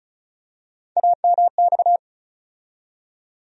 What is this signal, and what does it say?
Morse code, heard as a keyed tone, about 36 wpm, 700 Hz. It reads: AMX
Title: USA Call Sign Postfixes - 3 Letters 35wpm